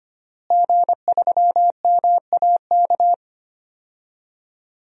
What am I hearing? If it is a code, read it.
Morse code, heard as a keyed tone, about 25 wpm, 700 Hz. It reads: G3MAK
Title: DXCC-100 Call Signs - 5 Characters - Repeated 2x - 25wpm